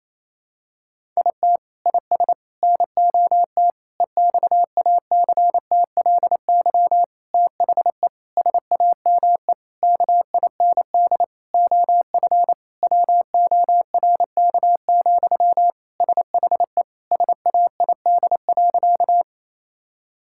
Morse code, heard as a keyed tone, about 27 wpm, 700 Hz. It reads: IT IS NOT EXACTLY THE SAME KIND OF WORK, SHE SAID.
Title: Sentences from Top 1000 Words - Review Part 6 - 28wpm